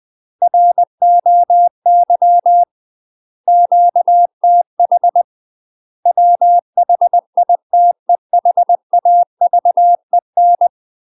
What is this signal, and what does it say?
Morse code, heard as a keyed tone, about 20 wpm, 700 Hz. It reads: ROY QTH WHITEHAVEN
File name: UK MASTER QSOs TRAINING / 20WPM / UK MASTER QSOs 0038